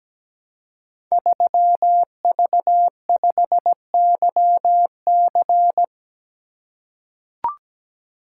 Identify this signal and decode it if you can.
Morse code, heard as a keyed tone, about 17 wpm, 700 Hz. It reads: 3V5YC
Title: Code Groups - 5 Letter:Numbers - 17wpm